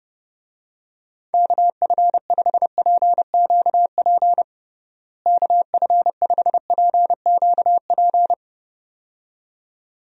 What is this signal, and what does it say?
Morse code, heard as a keyed tone, about 30 wpm, 700 Hz. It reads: KF5PQP KF5PQP
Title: USA Call Signs - 6 Characters - Repeated 2x - 30wpm